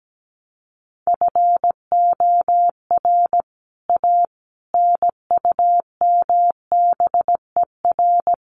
Morse code, heard as a keyed tone, about 17 wpm, 700 Hz. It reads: FOR A NUMBER
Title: Sets of 5 Words Encore 17wpm